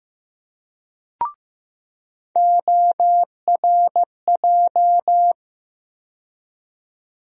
Morse code, heard as a keyed tone, about 15 wpm, 700 Hz. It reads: ORJ
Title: USA Call Sign Postfixes - 3 Letters 15wpm